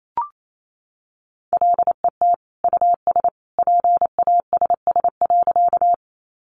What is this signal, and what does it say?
Morse code, heard as a keyed tone, about 28 wpm, 700 Hz. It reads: LET US PASS.
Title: Sentences from Top 400 Words 28wpm